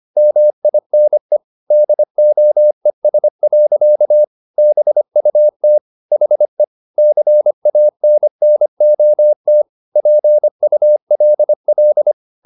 Morse code, heard as a keyed tone, about 25 wpm, 600 Hz. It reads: MINE DOES. BUT HE CANNOT PULL
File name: the old man and the sea part 1 25wpm